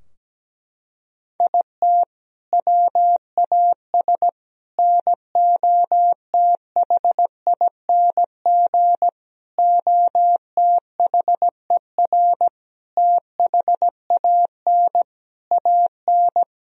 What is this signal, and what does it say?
Morse code, heard as a keyed tone, about 17 wpm, 700 Hz. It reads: IT WAS NOTHING OTHER THAN AN
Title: Sentences from Top 1000 Words - Review Part 2 - 17wpm